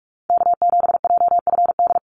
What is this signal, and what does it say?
Morse code, heard as a keyed tone, about 44 wpm, 700 Hz. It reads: K7JFD
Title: NA SOTA Activator Callsigns 45wpm